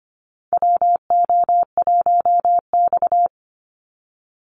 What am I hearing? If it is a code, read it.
Morse code, heard as a keyed tone, about 25 wpm, 700 Hz. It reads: WO1X
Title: Call Signs Repeated 2x - 25wpm